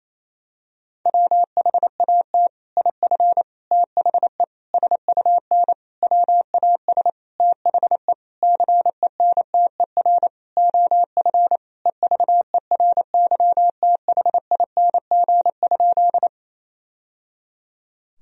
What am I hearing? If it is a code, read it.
Morse code, heard as a keyed tone, about 27 wpm, 700 Hz. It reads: WHAT IF THE SUN WAS THE CENTER OF EVERYTHING?
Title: Sentences from Top 1000 Words - Review Part 2 - 28wpm